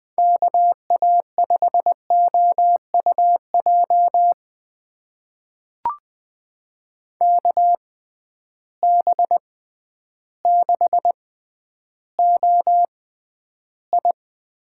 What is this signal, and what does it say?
Morse code, heard as a keyed tone, about 20 wpm, spaced as for 11 wpm, 700 Hz. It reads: KA5OUJ KB6OI
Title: USA Call Signs - 6 Characters - 6x Character Spacing - 20wpm